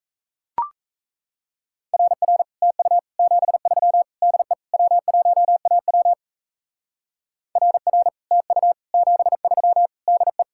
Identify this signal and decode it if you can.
Morse code, heard as a keyed tone, about 34 wpm, 700 Hz. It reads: RR TU 73 DE W1AW RR TU 73 DE
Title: Speed-Racing Common QSO Phrases - 28wpm